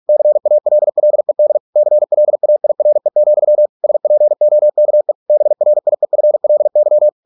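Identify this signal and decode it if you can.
Morse code, heard as a keyed tone, about 48 wpm, 600 Hz. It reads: CALLED CLAIRE, SPOKE BRIEFLY
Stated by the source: Morse practice at 45 wpm, the novel Never by Ken Follett in English, from CW med Gnister.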